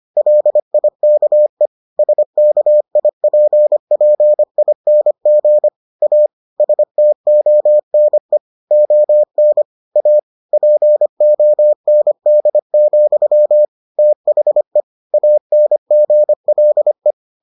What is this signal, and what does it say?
Morse code, heard as a keyed tone, about 25 wpm, 600 Hz. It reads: LIKE SKIPPING A STONE ON A POND, THE ANGLE